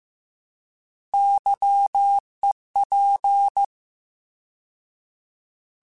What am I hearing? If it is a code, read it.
Morse code, heard as a keyed tone, about 15 wpm, 800 Hz. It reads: YEP